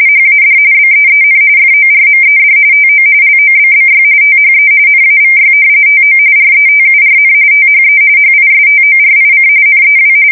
RTTY, about 45.45 baud. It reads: HAD BEEN FORMED TO LOBBY THE NAVY TO RETURN THE HISTORIC NEW
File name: fsk170 / battleshipnj